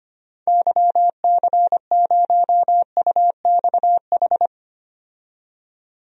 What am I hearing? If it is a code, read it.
Morse code, heard as a keyed tone, about 25 wpm, 700 Hz. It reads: YC0UXH
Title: DXCC-100 Call Signs - 6 Characters - 25wpm